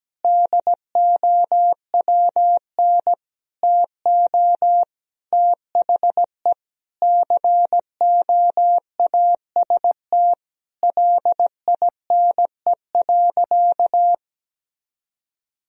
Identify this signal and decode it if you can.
Morse code, heard as a keyed tone, about 17 wpm, 700 Hz. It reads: DOWN TO THE COAST LINE.
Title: Sentences from Top 1000 Words - Review Part 4 - 17wpm